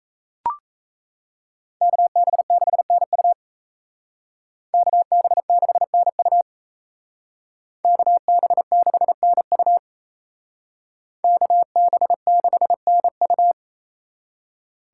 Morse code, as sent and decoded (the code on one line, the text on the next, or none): -.- -... -.... -. ..- / -.- -... -.... -. ..- / -.- -... -.... -. ..- / -.- -... -.... -. ..-
KB6NU KB6NU KB6NU KB6NU